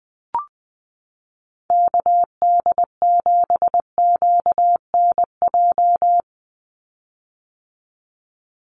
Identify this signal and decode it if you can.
Morse code, heard as a keyed tone, about 20 wpm, 700 Hz. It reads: KD7QNJ
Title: USA Call Signs 20wpm